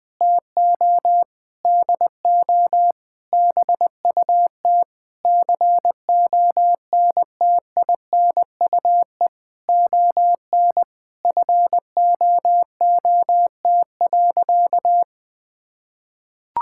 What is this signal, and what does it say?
Morse code, heard as a keyed tone, about 20 wpm, 700 Hz. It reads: TO DO BUT CONTINUE ON FOOT.
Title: Sentences from Top 1000 Words - Review Part 6 - 20wpm